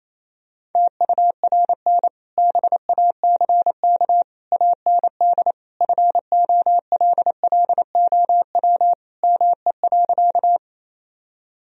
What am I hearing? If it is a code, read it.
Morse code, heard as a keyed tone, about 28 wpm, 700 Hz. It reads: TURN BACK AND FOLLOW ME.
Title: Sentences from Top 1000 Words - Review Part 1 - 28wpm